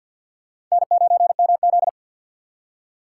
Morse code, heard as a keyed tone, about 47 wpm, 700 Hz. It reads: N9MZ
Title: USA Call Signs - 3x Character Spacing - 50wpm